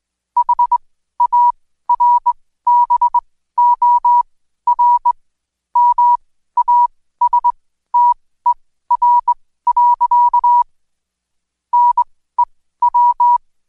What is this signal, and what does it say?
Morse code, heard as a keyed tone, about 21 wpm, spaced as for 14 wpm, 970 Hz. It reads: HARBOR MASTER. NEW
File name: Code Quick / AudioDownld2 / Seg123-Day28-DayDrill 28